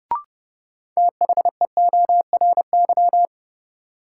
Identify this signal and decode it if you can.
Morse code, heard as a keyed tone, about 30 wpm, 700 Hz. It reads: THEORY
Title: Sentences from Top 600 Words - New Word Drill - 30wpm